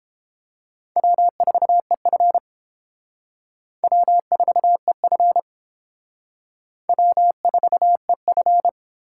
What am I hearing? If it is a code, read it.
Morse code, heard as a keyed tone, about 29 wpm, 700 Hz. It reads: W4EF W4EF W4EF